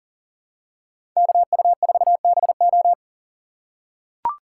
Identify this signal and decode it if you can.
Morse code, heard as a keyed tone, about 40 wpm, 700 Hz. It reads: KU4BO
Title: USA Call Signs - 5 Characters - 6x Character Spacing - 40wpm